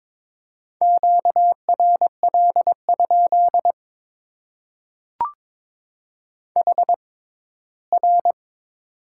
Morse code, heard as a keyed tone, about 22 wpm, spaced as for 15 wpm, 700 Hz. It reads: QRL? HR
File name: Top 90 QSO Elements - 6x Character Spacing - 22wpm